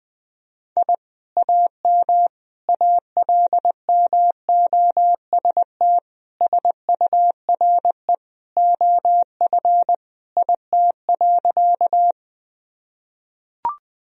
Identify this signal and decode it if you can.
Morse code, heard as a keyed tone, about 20 wpm, 700 Hz. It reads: I AM ALMOST SURE OF IT.
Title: Sentences from Top 1000 Words - Review Part 4 - 20wpm